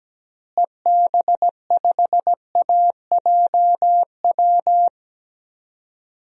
Morse code, heard as a keyed tone, about 17 wpm, 700 Hz. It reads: EB5AJW
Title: DXCC-100 Call Signs - 6 Characters - Repeated 3x - 17wpm